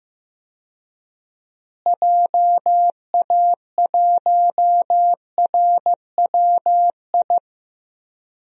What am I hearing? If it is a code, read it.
Morse code, heard as a keyed tone, about 16 wpm, 700 Hz. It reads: JA1RWI
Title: USA Call Signs - 6 Characters - Repeated 2x - 15wpm